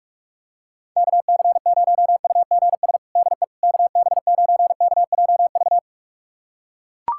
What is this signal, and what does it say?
Morse code, heard as a keyed tone, about 41 wpm, 700 Hz. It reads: KC0UGS DE KB9KJV
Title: USA Callsign DE Callsign - 6 Characters - 45wpm